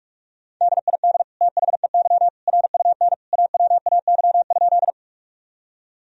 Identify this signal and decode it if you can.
Morse code, heard as a keyed tone, about 46 wpm, 700 Hz. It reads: DID THEY RUN AWAY?
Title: Sentences from Top 1000 Words - Review Part 1 - 45wpm